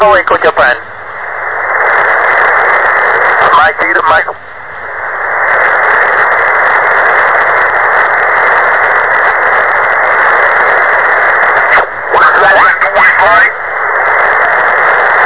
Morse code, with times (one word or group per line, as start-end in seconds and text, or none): none